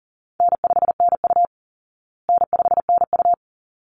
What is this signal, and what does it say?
Morse code, heard as a keyed tone, about 40 wpm, 700 Hz. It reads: N5NU N5NU
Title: Easy Call Signs Repeated 2x - 40wpm